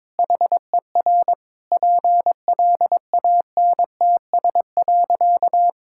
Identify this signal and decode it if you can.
Morse code, heard as a keyed tone, about 22 wpm, 700 Hz. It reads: HER PLANTS.
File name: practice / 22wpm / Sentences with 4-Word Phrases - 22wpm